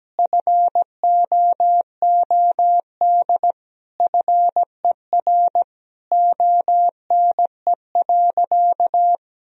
Morse code, heard as a keyed tone, about 17 wpm, 700 Hz. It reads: FOOD FER ONE.
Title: Sentences from Top 300 Words and Abbreviations 17wpm